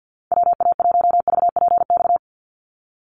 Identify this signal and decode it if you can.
Morse code, heard as a keyed tone, about 48 wpm, 700 Hz. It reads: WA1VPX
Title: USA Call Signs 50wpm